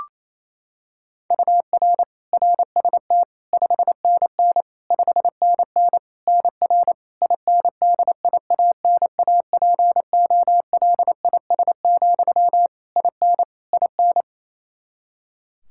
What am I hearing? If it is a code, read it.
Morse code, heard as a keyed tone, about 27 wpm, 700 Hz. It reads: UR RST 5NN 5NN NR INDIANAPOLIS, IN IN
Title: K1USN SST QSO Practice 28wpm